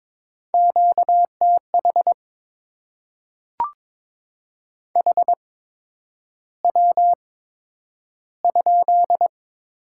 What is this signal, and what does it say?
Morse code, heard as a keyed tone, about 22 wpm, spaced as for 9 wpm, 700 Hz. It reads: QTH HW?
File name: Top 20 QSO Elements - 8x Character Spacing - 22wpm